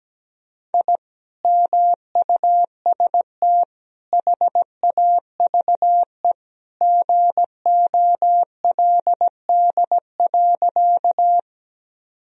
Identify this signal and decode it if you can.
Morse code, heard as a keyed tone, about 17 wpm, 700 Hz. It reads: I MUST HAVE GOLD.
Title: Sentences from Top 1000 Words - Review Part 3 - 17wpm